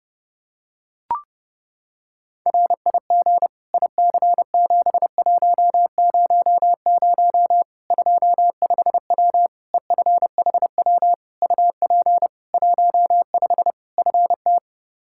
Morse code, heard as a keyed tone, about 30 wpm, 700 Hz. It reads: RIG IC7100 25W EFHW UP 15 FT
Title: Practice QSO - Exchange 2 - Info - 30wpm